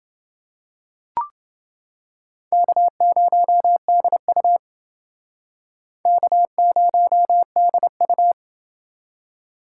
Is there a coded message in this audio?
Yes: Morse code from 2.5 to 8.3 s, heard as a keyed tone, about 28 wpm, 700 Hz. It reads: K0DU K0DU